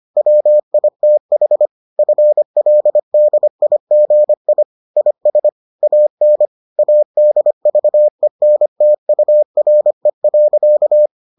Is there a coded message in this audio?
Yes: Morse code from 0.2 to 11.1 s, heard as a keyed tone, about 25 wpm, 600 Hz. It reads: WITH FLDIGI IS AN ADVENTURE.